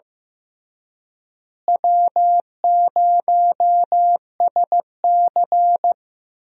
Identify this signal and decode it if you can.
Morse code, heard as a keyed tone, about 16 wpm, 700 Hz. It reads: W0SC
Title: USA Call Signs - 4 Characters - Repeated 3x - 15wpm